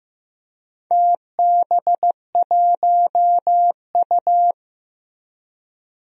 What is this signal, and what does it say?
Morse code, heard as a keyed tone, about 16 wpm, 700 Hz. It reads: TB1U
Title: DXCC-100 Call Signs - 4 Characters - Repeated 2x - 15wpm